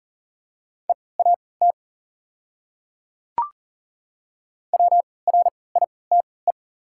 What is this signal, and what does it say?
Morse code, heard as a keyed tone, about 40 wpm, spaced as for 23 wpm, 700 Hz. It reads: EAT WRITE
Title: 3x Character Spacing - Top 300 Words - 40wpm